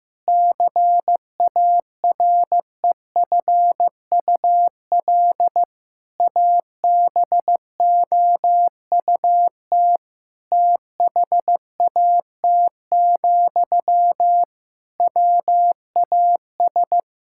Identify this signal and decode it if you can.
Morse code, heard as a keyed tone, about 15 wpm, 700 Hz. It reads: CAREFUL ABOUT THAT, WAS
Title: Sentences from Top 1600 Words 15wpm